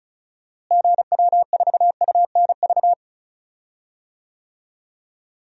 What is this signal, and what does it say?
Morse code, heard as a keyed tone, about 36 wpm, 700 Hz. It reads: GW4UNV
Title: DXCC-100 Call Signs - 6 Characters - Repeated 3x - 35wpm